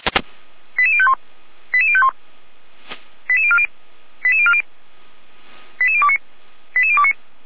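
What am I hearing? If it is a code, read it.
Five-tone selective call (ZVEI), heard as quick runs of five tones, tones of about 70 ms. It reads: 88051 88051 88040 88040 88029 88029
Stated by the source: ZVEI five-tone call received by SDR.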